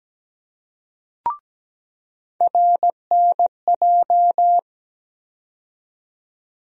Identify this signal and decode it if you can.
Morse code, heard as a keyed tone, about 17 wpm, 700 Hz. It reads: RNJ
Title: USA Call Sign Postfixes - 3 Letters 17wpm